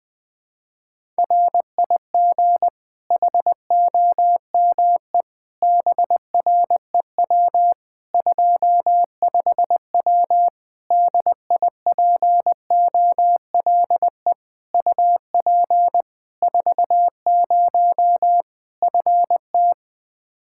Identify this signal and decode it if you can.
Morse code, heard as a keyed tone, about 20 wpm, 700 Hz. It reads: RIG HOME BREW 25W DIPOLE UP 40 FT
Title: K1USN SST QSO Practice 20wpm